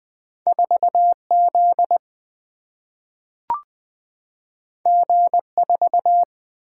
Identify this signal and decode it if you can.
Morse code, heard as a keyed tone, about 20 wpm, 700 Hz. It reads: 4Z G4
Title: Single-Letter Single-Number 20wpm